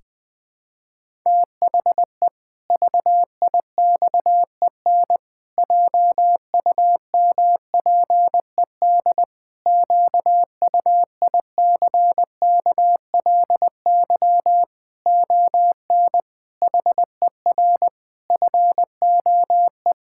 Morse code, heard as a keyed tone, about 21 wpm, 700 Hz. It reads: THE VIXEN JUMPED QUICKLY ON HER FOE
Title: Pangram Challenge v2 - 20wpm